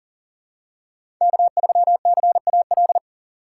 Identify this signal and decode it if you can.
Morse code, heard as a keyed tone, about 40 wpm, 700 Hz. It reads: K3CAL
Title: USA Call Signs - 5 Characters - 4x Character Spacing - 40wpm